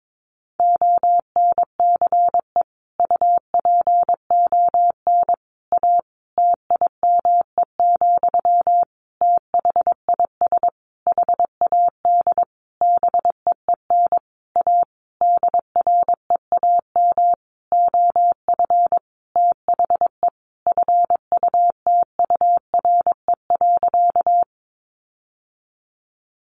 Morse code, heard as a keyed tone, about 22 wpm, 700 Hz. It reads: ONCE UPON A TIME, THIS HAD BEEN A DREAM OF THE FUTURE.
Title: Sentences from Top 1500 Words 22wpm